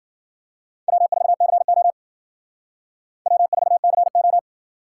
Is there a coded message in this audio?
Yes: Morse code from 0.9 to 4.4 s, heard as a keyed tone, about 52 wpm, 700 Hz. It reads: W4CO W4CO